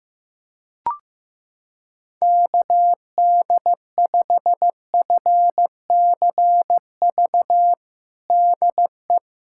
Morse code, heard as a keyed tone, about 15 wpm, 700 Hz. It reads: KD5FCV DE
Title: USA Callsign DE Callsign 15wpm